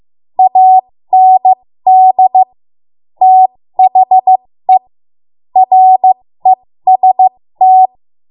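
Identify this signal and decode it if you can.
Morse code, heard as a keyed tone, about 16 wpm, spaced as for 13 wpm, 750 Hz. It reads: AND THE REST
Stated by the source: 13 wpm Morse code practice archive transmission.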